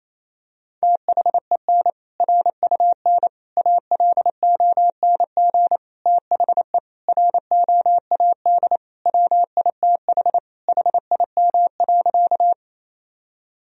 Morse code, heard as a keyed tone, about 28 wpm, 700 Hz. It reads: THEN RUN ALONG THE ROAD WITH HIM.
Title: Sentences from Top 1000 Words - Review Part 2 - 28wpm